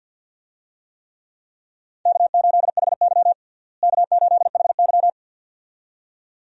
Morse code, heard as a keyed tone, about 47 wpm, 700 Hz. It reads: K8HY K8HY
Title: USA Call Signs - 4 Characters - Repeated 2x - 50wpm